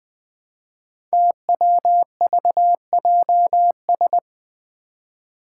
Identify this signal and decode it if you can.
Morse code, heard as a keyed tone, about 20 wpm, 700 Hz. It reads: TWVJS
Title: Five Challenging Letter-Number Combinations 20wpm